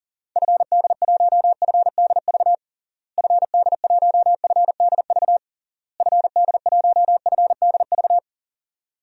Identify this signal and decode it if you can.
Morse code, heard as a keyed tone, about 40 wpm, 700 Hz. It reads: FD1FDV FD1FDV FD1FDV
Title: DXCC-100 Call Signs - 6 Characters - Repeated 3x - 40wpm